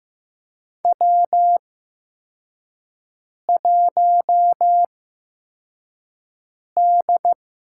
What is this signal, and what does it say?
Morse code, heard as a keyed tone, about 15 wpm, spaced as for 4 wpm, 700 Hz. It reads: W1D
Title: USA Call Signs - 4 Characters - 8x Character Spacing - 15wpm